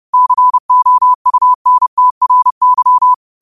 Morse code, heard as a keyed tone, about 30 wpm, 1000 Hz. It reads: COUNTRY
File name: qotdfiles / 30wpm